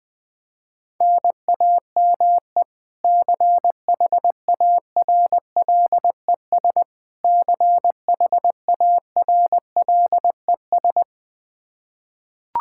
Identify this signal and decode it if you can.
Morse code, heard as a keyed tone, about 21 wpm, 700 Hz. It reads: NAME CHARLES CHARLES
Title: Practice QSO - Exchange 1 - Beginning - 20wpm